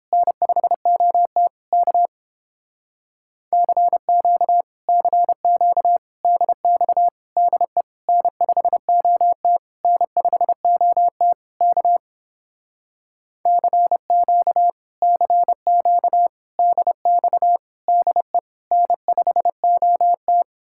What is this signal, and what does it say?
Morse code, heard as a keyed tone, about 29 wpm, 700 Hz. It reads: N5OT K CQ CQ DX DE N5OT N5OT K CQ CQ DX DE N5OT